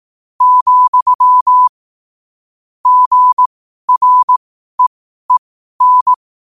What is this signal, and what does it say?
Morse code, heard as a keyed tone, about 18 wpm, spaced as for 12 wpm, 1000 Hz. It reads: , GREEN
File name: qotdfiles / 12wpm